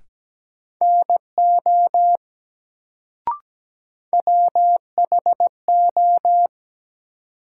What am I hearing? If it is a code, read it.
Morse code, heard as a keyed tone, about 17 wpm, 700 Hz. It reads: NO WHO
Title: Sentences from Top 100 Words - New Word Drill - 17wpm